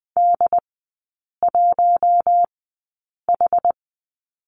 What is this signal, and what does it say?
Morse code, heard as a keyed tone, about 20 wpm, spaced as for 8 wpm, 700 Hz. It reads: D1H
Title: 127 Lesson 21e - Number 6 - Three Characters 20wpm